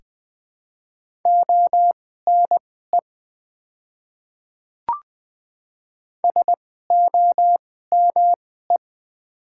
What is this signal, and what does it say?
Morse code, heard as a keyed tone, about 20 wpm, spaced as for 14 wpm, 700 Hz. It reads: ONE SOME
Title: 2x Character Spacing - Top 100 Words - 20wpm